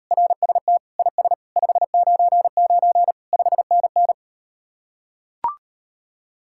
Morse code, heard as a keyed tone, about 38 wpm, 700 Hz. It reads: RST IS 599 5NN